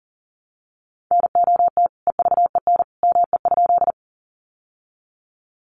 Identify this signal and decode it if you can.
Morse code, heard as a keyed tone, about 40 wpm, 700 Hz. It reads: NOT EVEN ME?
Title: Sentences from Top 600 Words 40wpm